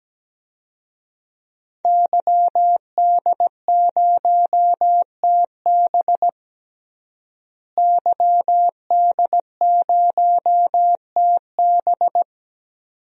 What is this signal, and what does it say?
Morse code, heard as a keyed tone, about 17 wpm, 700 Hz. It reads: YD0TB YD0TB
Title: DXCC-100 Call Signs - 5 Characters - Repeated 2x - 17wpm